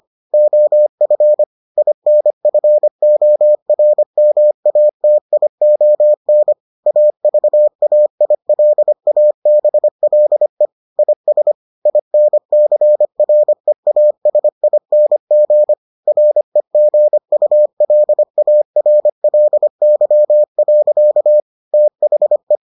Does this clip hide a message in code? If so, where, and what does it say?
Yes: Morse code from 0.3 to 22.6 s, heard as a keyed tone, about 25 wpm, 600 Hz. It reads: OF INFORMATION AVAILABLE IS INCREASING REGULARLY. THE